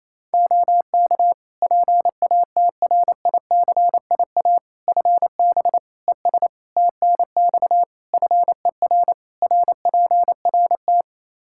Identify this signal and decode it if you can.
Morse code, heard as a keyed tone, about 28 wpm, 700 Hz. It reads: OK PATRICIA FB ES TNX FER RPRT
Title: Practice QSO - Exchange 2 - Info - 28wpm